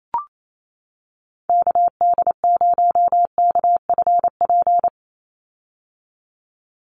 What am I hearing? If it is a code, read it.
Morse code, heard as a keyed tone, about 28 wpm, 700 Hz. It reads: KD0KFP
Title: USA Call Signs 28wpm